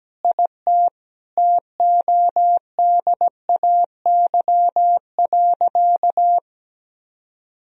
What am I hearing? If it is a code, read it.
Morse code, heard as a keyed tone, about 17 wpm, 700 Hz. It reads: IT TODAY.
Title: Sentences from Top 1000 Words - Review Part 4 - 17wpm